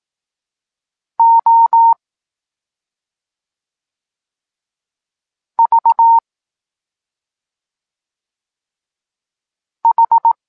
Morse code, heard as a keyed tone, about 18 wpm, spaced as for 2 wpm, 920 Hz. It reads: OVH
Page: Tekst 8 – doskonalenie odbioru poznanych liter